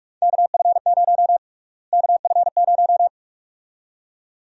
Morse code, heard as a keyed tone, about 46 wpm, 700 Hz. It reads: KF0 KF0
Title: USA Call Sign Prefixes - Repeated 2x - 45wpm